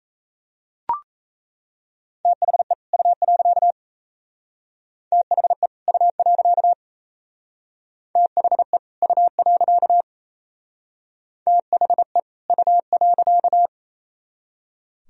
Morse code, heard as a keyed tone, about 34 wpm, 700 Hz. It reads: THE U. THE U. THE U. THE U.
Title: Speed-Racing - Sets of 2 Words 28wpm